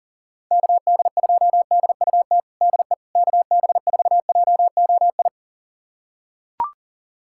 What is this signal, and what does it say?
Morse code, heard as a keyed tone, about 40 wpm, 700 Hz. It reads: KD2DUT DE KB4JOI
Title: USA Callsign DE Callsign - 6 Characters - 40wpm